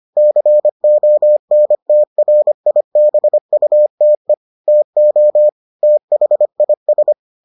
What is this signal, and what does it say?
Morse code, heard as a keyed tone, about 25 wpm, 600 Hz. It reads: CONTRIBUTE TO THIS